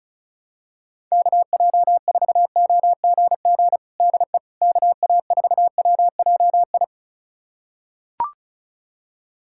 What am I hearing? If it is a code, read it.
Morse code, heard as a keyed tone, about 36 wpm, 700 Hz. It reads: KJ4OGG DE KA4WJI
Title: USA Callsign DE Callsign - 6 Characters - 35wpm